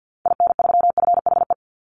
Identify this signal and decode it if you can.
Morse code, heard as a keyed tone, about 50 wpm, 700 Hz. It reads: IN3FHE
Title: Call Signs Repeated 2x - 50wpm